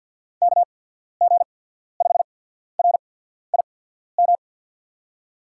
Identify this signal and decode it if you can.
Morse code, heard as a keyed tone, about 49 wpm, spaced as for 14 wpm, 700 Hz. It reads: KG5RIM
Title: USA Call Signs - 6 Characters - 8x Character Spacing - 50wpm